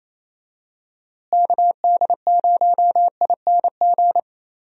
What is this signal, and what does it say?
Morse code, heard as a keyed tone, about 28 wpm, 700 Hz. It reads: KD0ING
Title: USA Call Signs - 6 Characters - Repeated 2x - 28wpm